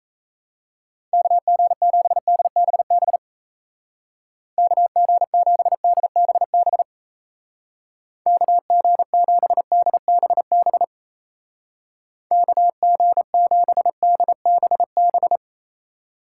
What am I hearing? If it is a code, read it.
Morse code, heard as a keyed tone, about 36 wpm, 700 Hz. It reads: KG7DBB KG7DBB KG7DBB KG7DBB